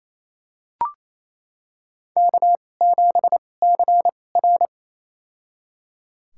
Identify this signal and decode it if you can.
Morse code, heard as a keyed tone, about 28 wpm, spaced as for 20 wpm, 700 Hz. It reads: K7CR
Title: USA Call Signs - 4 Characters - 2x Character Spacing - 28wpm